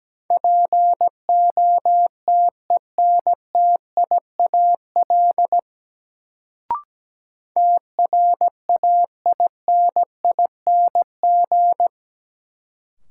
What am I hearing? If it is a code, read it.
Morse code, heard as a keyed tone, about 17 wpm, 700 Hz. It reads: POTENTIAL TRAINING
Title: Sentences from Top 1000 Words - Review Part 4 - New Word Drill - 17wpm